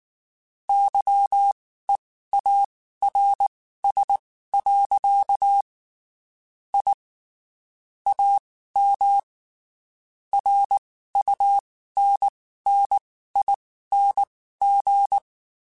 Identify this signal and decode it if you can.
Morse code, heard as a keyed tone, about 19 wpm, spaced as for 14 wpm, 800 Hz. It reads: YEARS. I AM RUNNING
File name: (spaced) / QSOs Spaced-mp3s / kc7sac @18 sp